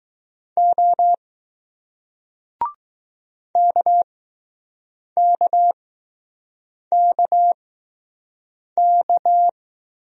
Morse code, heard as a keyed tone, about 20 wpm, spaced as for 6 wpm, 700 Hz. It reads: O KKKK